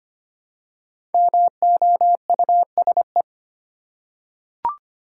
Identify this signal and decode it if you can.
Morse code, heard as a keyed tone, about 25 wpm, 700 Hz. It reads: MOUSE
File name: practice / 25wpm / Easy Words with Hints - 25wpm